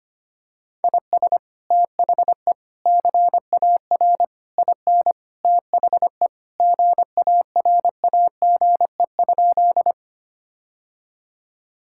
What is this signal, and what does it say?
Morse code, heard as a keyed tone, about 25 wpm, 700 Hz. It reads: IS THE CAR IN THE GARAGE?
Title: Sentences from Top 1600 Words 25wpm